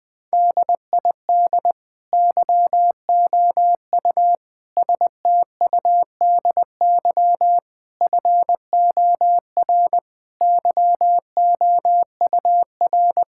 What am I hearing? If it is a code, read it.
Morse code, heard as a keyed tone, about 21 wpm, 700 Hz. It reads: DID YOU STUDY FOR YOUR
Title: Sentences from Top 1000 Words - Review Part 6 - 20wpm